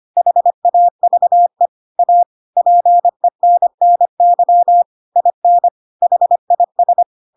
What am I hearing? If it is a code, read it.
Morse code, heard as a keyed tone, about 25 wpm, 700 Hz. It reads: HAVE A PENNY IN HIS
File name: Pinocchio / Pinocchio ch8 25 25 ik1hge